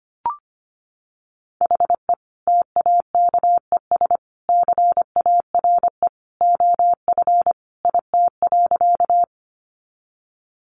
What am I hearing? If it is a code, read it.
Morse code, heard as a keyed tone, about 25 wpm, 700 Hz. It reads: HE TAKES CARE OF IT.
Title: Additional Easy Phrases 25wpm